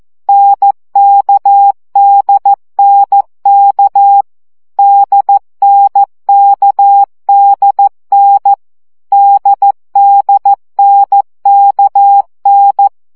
Morse code, heard as a keyed tone, about 14 wpm, 800 Hz. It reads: NKDNK DNKDN DDNKN
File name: CW gyakorlas n d k 12 baud